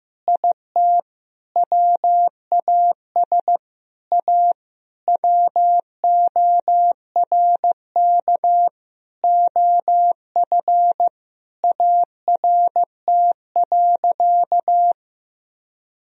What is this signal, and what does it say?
Morse code, heard as a keyed tone, about 15 wpm, 700 Hz. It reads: IT WAS A WORK OF ART.
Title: Sentences from Top 1000 Words - Review Part 3 - 15wpm